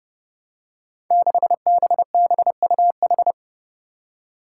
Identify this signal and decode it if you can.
Morse code, heard as a keyed tone, about 32 wpm, 700 Hz. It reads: 6BBUH
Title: Five Challenging Letter-Number Combinations 30wpm